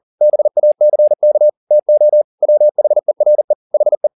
Morse code, heard as a keyed tone, about 40 wpm, 600 Hz. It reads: BACK TO WHERE HE
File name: the old man and the sea 2 40wpm